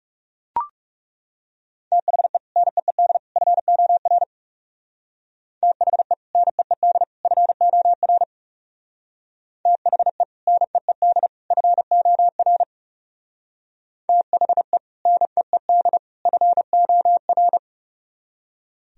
Morse code, heard as a keyed tone, about 38 wpm, 700 Hz. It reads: THE NEED FOR THE NEED FOR THE NEED FOR THE NEED FOR